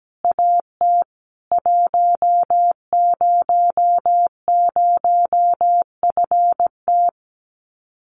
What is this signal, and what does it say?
Morse code, heard as a keyed tone, about 17 wpm, 700 Hz. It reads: AT 100FT
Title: Common QSO Phrases 17wpm